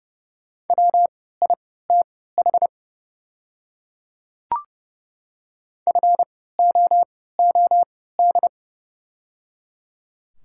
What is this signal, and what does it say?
Morse code, heard as a keyed tone, about 30 wpm, spaced as for 17 wpm, 690 Hz. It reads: WITH FOOD
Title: 3x Character Spacing - Top 300 Words - 30wpm